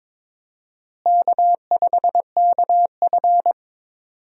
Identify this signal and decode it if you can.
Morse code, heard as a keyed tone, about 22 wpm, 700 Hz. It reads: K5KF
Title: USA Call Signs - 4 Characters - Repeated 3x - 22wpm